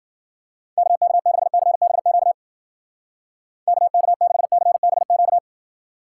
Morse code, heard as a keyed tone, about 52 wpm, 700 Hz. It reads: KK6CBQ KK6CBQ